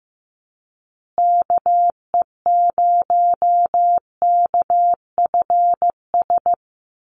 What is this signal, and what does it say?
Morse code, heard as a keyed tone, about 15 wpm, 700 Hz. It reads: KE0KFS
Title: USA Call Signs Repeated 2x 15wpm